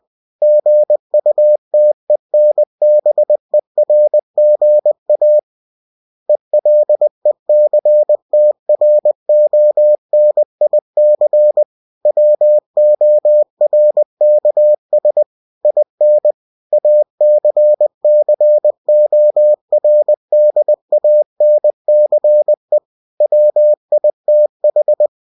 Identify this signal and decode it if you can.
Morse code, heard as a keyed tone, about 20 wpm, 600 Hz. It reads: GUTENBERGA ELECTRONIC WORKS IN ACCORDANCE WITH
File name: cw-books / winnie-the-pooh / winnie-20wpm-page-0120